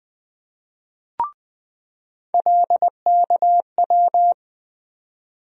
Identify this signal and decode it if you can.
Morse code, heard as a keyed tone, about 20 wpm, 700 Hz. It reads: LKW
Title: USA Call Sign Postfixes - 3 Letters 20wpm